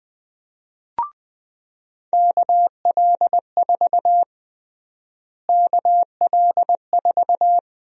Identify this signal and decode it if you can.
Morse code, heard as a keyed tone, about 20 wpm, 700 Hz. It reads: KL4 KL4
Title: USA Call Sign Prefixes - Repeated 2x - 20wpm